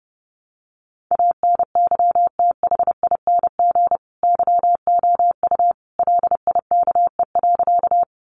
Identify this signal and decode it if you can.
Morse code, heard as a keyed tone, about 30 wpm, 700 Hz. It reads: ANYTHING YOU LIKE.
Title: Sentences from Top 1000 Words 30wpm